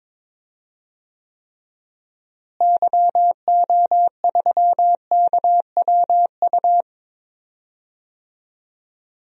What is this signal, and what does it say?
Morse code, heard as a keyed tone, about 22 wpm, 700 Hz. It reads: YO3KWU
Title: DXCC-100 Call Signs - 6 Characters - Repeated 2x - 22wpm